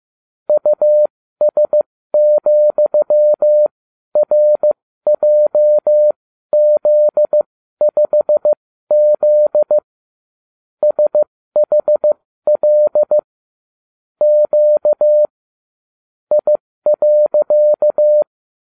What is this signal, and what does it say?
Morse code, heard as a keyed tone, about 15 wpm, spaced as for 12 wpm, 600 Hz. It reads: US,RJZ5Z SHL Q I.
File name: ubacw36